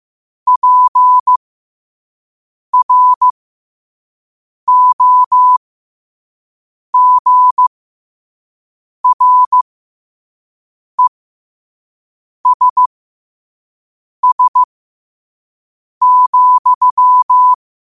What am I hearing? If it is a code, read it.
Morse code, heard as a keyed tone, about 15 wpm, spaced as for 5 wpm, 1000 Hz. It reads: PROGRESS,
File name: qotdfiles / 5wpm